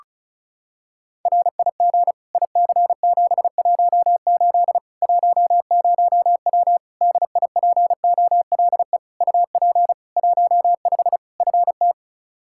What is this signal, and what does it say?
Morse code, heard as a keyed tone, about 36 wpm, 700 Hz. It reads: RIG IC718 10W DIPOLE UP 15 FT
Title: Practice QSO - Exchange 2 - Info - 35wpm